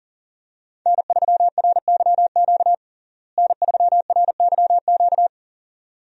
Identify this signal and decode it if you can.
Morse code, heard as a keyed tone, about 40 wpm, 700 Hz. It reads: N3RYQ N3RYQ
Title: USA Call Signs - 5 Characters - Repeated 2x - 40wpm